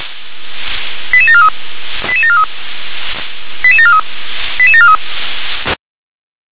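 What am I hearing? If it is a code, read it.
Five-tone selective call (ZVEI), heard as quick runs of five tones, tones of about 70 ms. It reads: 88743 88743 88743 88743
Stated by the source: ZVEI five-tone selective call 88819 received by SDR.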